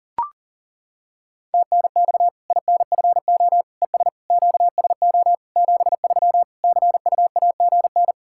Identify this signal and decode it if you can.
Morse code, heard as a keyed tone, about 40 wpm, 700 Hz. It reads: TNX INFO ES QSO 73 CUAGN
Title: Practice QSO - Exchange 4 - Ending - 40wpm